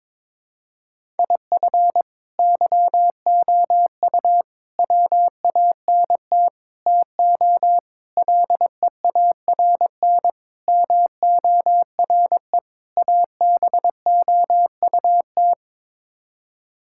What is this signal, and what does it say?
Morse code, heard as a keyed tone, about 22 wpm, 700 Hz. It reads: IF YOU WANT TO LEARN MORE ABOUT
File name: practice / 22wpm / Sets of 7 Words - 22wpm